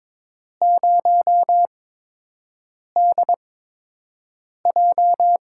Morse code, heard as a keyed tone, about 22 wpm, 700 Hz. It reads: 0 D J
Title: USA Call Signs - 6 Characters - 8x Character Spacing - 22wpm